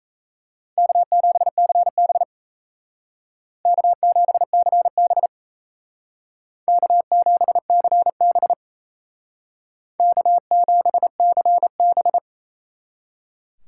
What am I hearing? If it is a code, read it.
Morse code, heard as a keyed tone, about 34 wpm, 700 Hz. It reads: K7CB K7CB K7CB K7CB
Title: Speed Racing - USA 1x2 Call Signs 28wpm